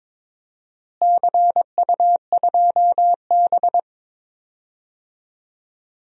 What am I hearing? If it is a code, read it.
Morse code, heard as a keyed tone, about 21 wpm, 700 Hz. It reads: CU2B
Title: DXCC-100 Call Signs - 4 Characters - Repeated 3x - 22wpm